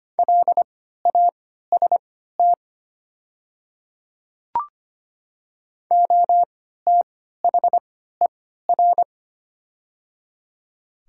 Morse code, heard as a keyed tone, about 25 wpm, spaced as for 14 wpm, 700 Hz. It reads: LAST OTHER
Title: 3x Character Spacing - Top 300 Words - 25wpm